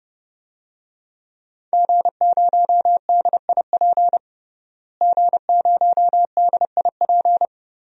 Morse code, heard as a keyed tone, about 30 wpm, 700 Hz. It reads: G0DIP G0DIP
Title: DXCC-100 Call Signs - 5 Characters - Repeated 2x - 30wpm